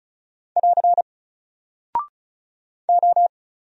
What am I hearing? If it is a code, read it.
Morse code, heard as a keyed tone, about 36 wpm, 700 Hz. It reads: <AR> O
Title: Single Character (Letter:Number:Punctuation:Prosign) Alt 35wpm